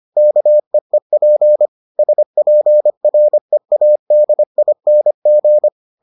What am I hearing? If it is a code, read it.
Morse code, heard as a keyed tone, about 25 wpm, 600 Hz. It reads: KEEP SPREADING